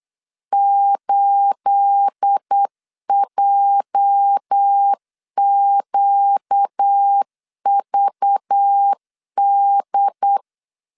Morse code, heard as a keyed tone, about 8 wpm, 800 Hz. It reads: 8JQVD
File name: CW / DET / 1 9 1500C S C 8wpm 5653